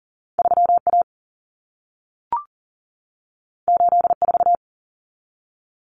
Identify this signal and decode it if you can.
Morse code, heard as a keyed tone, about 40 wpm, 700 Hz. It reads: 3A 84
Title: Common Cut Numbers - 2-Digit 40wpm